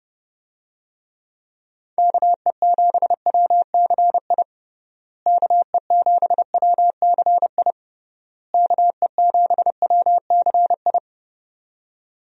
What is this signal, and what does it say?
Morse code, heard as a keyed tone, about 30 wpm, 700 Hz. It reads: KE7WCI KE7WCI KE7WCI
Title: USA Call Signs - 6 Characters - Repeated 3x - 30wpm